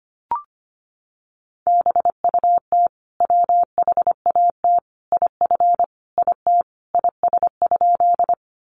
Morse code, heard as a keyed tone, about 25 wpm, 700 Hz. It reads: BUT WHAT IF IT IS?
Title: Sentences from Top 100 Words 25wpm